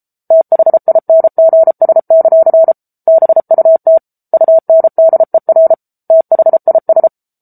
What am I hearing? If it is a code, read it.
Morse code, heard as a keyed tone, about 34 wpm, 650 Hz. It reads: THINGS; BUT UNDER THIS